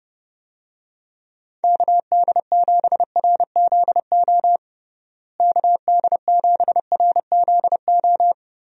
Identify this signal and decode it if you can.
Morse code, heard as a keyed tone, about 32 wpm, 700 Hz. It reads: KD7RZO KD7RZO
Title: USA Call Signs - 6 Characters - Repeated 3x - 30wpm